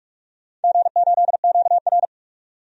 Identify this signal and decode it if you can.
Morse code, heard as a keyed tone, about 41 wpm, 700 Hz. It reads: G8QR
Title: DXCC-100 Call Signs - 4 Characters - 45wpm